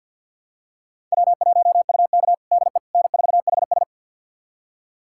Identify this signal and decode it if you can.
Morse code, heard as a keyed tone, about 48 wpm, 700 Hz. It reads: W1UK DE N4HS
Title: USA Callsign DE Callsign - 4 Characters - 50wpm